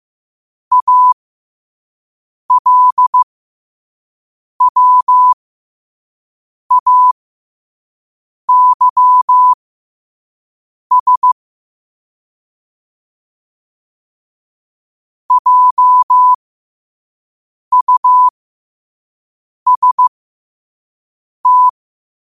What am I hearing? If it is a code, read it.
Morse code, heard as a keyed tone, about 15 wpm, spaced as for 5 wpm, 1000 Hz. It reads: ALWAYS JUST
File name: qotdfiles / 5wpm